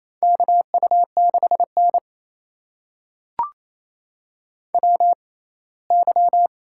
Morse code, heard as a keyed tone, about 28 wpm, spaced as for 19 wpm, 700 Hz. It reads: KU6N WY
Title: USA Call Signs - 4 Characters - 6x Character Spacing - 28wpm